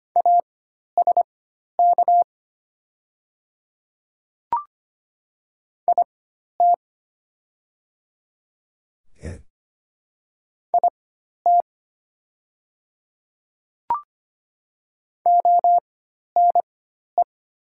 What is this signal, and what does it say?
Morse code, heard as a keyed tone, about 25 wpm, spaced as for 12 wpm, 690 Hz. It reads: ASK IT IT ONE
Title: 4x Character Spacing - Top 200 Words - 25wpm